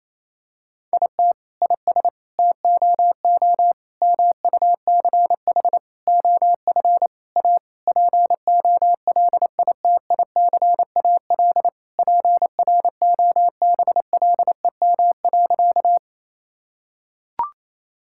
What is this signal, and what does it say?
Morse code, heard as a keyed tone, about 28 wpm, 700 Hz. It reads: IT IS TOO MUCH OF A POLITICAL PROBLEM.
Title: Sentences from Top 1000 Words - Review Part 5 - 28wpm